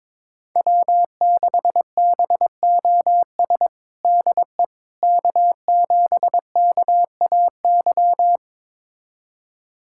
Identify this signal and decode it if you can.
Morse code, heard as a keyed tone, about 22 wpm, 700 Hz. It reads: W6BOS DE K7KAY
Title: USA Callsign DE Callsign - 5 Characters - 22wpm